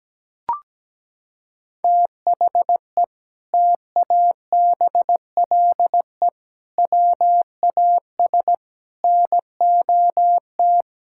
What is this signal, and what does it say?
Morse code, heard as a keyed tone, about 17 wpm, 700 Hz. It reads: THE TABLE WAS NOT
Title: Sentences from Top 1000 Words - Review Part 6 - 17wpm